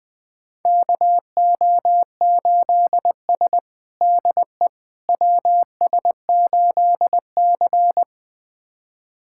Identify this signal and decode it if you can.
Morse code, heard as a keyed tone, about 20 wpm, 700 Hz. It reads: KO8S DE WS8C
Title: USA Callsign DE Callsign - 4 Characters - 20wpm